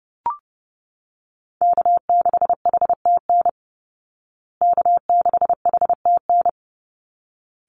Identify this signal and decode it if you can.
Morse code, heard as a keyed tone, about 30 wpm, 700 Hz. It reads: K6HTN K6HTN
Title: Call Signs Repeated 3x - 30wpm